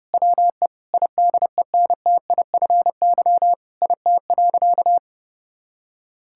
Morse code, heard as a keyed tone, about 28 wpm, 700 Hz. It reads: WE IDENTIFY IT.
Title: Sentences from Top 1000 Words - Review Part 6 - 30wpm